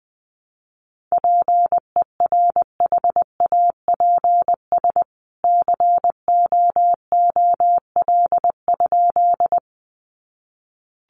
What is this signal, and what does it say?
Morse code, heard as a keyed tone, about 20 wpm, 700 Hz. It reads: PERHAPS COOL?
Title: Sentences from Top 1200 Words 20wpm